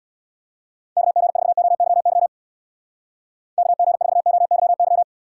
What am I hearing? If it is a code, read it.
Morse code, heard as a keyed tone, about 65 wpm, 700 Hz. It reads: KG4OYY KG4OYY